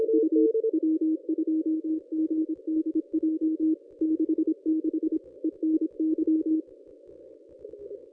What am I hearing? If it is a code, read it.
Morse code, heard as a keyed tone, about 26 wpm, 320 Hz. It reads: U W2GDJ 66 ENY
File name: contests / ss-cw-k5tr-op-n5rz -1000